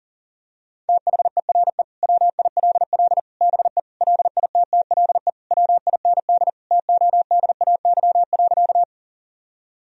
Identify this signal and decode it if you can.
Morse code, heard as a keyed tone, about 37 wpm, 700 Hz. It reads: THERE WILL BE LITTLE WIND TODAY.
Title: Sentences from Top 1000 Words - Review Part 2 - 40wpm